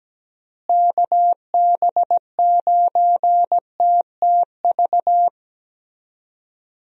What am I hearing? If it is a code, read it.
Morse code, heard as a keyed tone, about 17 wpm, 700 Hz. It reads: KB9TTV
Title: USA Call Signs - 6 Characters - Repeated 3x - 17wpm